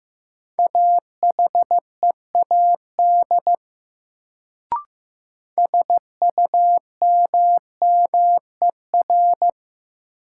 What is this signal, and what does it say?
Morse code, heard as a keyed tone, about 15 wpm, 700 Hz. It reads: AHEAD SUMMER
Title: Sentences from Top 1100 Words - New Word Drill Alt - 15wpm